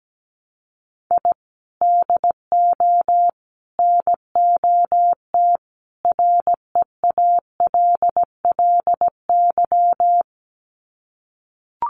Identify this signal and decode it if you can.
Morse code, heard as a keyed tone, about 17 wpm, 700 Hz. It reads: I DO NOT REALLY
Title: Sets of 4 Words 17wpm